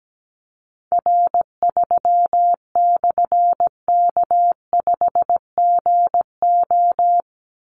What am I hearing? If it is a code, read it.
Morse code, heard as a keyed tone, about 17 wpm, 700 Hz. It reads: R3/K5GO
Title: Call Signs Repeated 3x - 17wpm